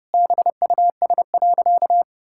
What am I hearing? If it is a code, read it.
Morse code, heard as a keyed tone, about 30 wpm, 700 Hz. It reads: BUS.
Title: Sentences from Top 1000 Words - Review Part 6 - 30wpm